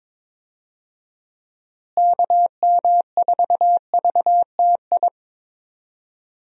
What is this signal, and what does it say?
Morse code, heard as a keyed tone, about 22 wpm, 700 Hz. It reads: KM4VTI
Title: USA Call Signs - 6 Characters - Repeated 2x - 22wpm